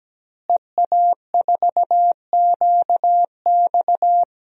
Morse code, heard as a keyed tone, about 17 wpm, 700 Hz. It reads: EA4QX
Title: DXCC-100 Call Signs - 5 Characters - Repeated 3x - 17wpm